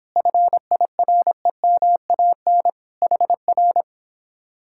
Morse code, heard as a keyed tone, about 31 wpm, 700 Hz. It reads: FIREMAN HR